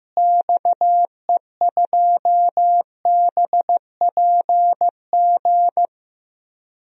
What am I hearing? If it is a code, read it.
Morse code, heard as a keyed tone, about 15 wpm, 700 Hz. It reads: XE2BPG
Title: DXCC-100 Call Signs - 6 Characters - Repeated 3x - 15wpm